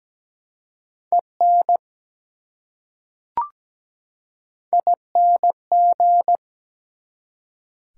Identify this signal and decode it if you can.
Morse code, heard as a keyed tone, about 17 wpm, 700 Hz. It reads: EN ING
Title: Prefixes and Suffixes 17wpm